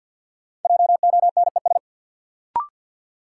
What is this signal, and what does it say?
Morse code, heard as a keyed tone, about 51 wpm, 700 Hz. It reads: JONES
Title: Sets of 5 Words - New Word Drill - 50wpm